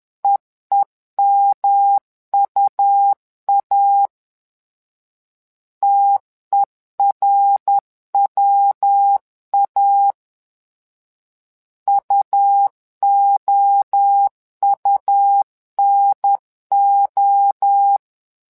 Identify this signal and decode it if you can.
Morse code, heard as a keyed tone, about 10 wpm, 800 Hz. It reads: EEMUA TERWA UOUNO
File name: LEZIONI / 3 MWRO / 3 MWRO 10wpm 2148